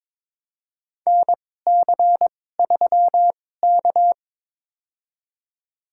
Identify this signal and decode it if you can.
Morse code, heard as a keyed tone, about 21 wpm, spaced as for 16 wpm, 700 Hz. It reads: NC3K
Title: USA Call Signs - 2x Character Spacing - 22wpm